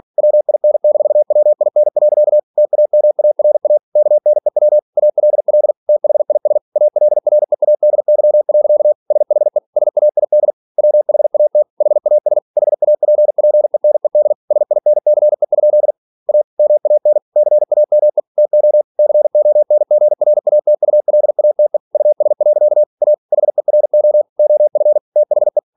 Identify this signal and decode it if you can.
Morse code, heard as a keyed tone, about 46 wpm, 600 Hz. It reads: WIN-WIN. TAMARA KNEW ALL THIS ALREADY. SHE SAID WHAT HAS HAPPENED SINCE? A MAN CAME TO CONGRATULATE US. A HERO OF THE
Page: Never 47wpm